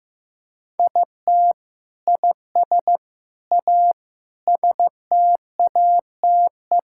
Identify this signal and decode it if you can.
Morse code, heard as a keyed tone, about 15 wpm, 700 Hz. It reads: IT IS A STATE
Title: Sentences from Top 1000 Words - Review Part 5 - 15wpm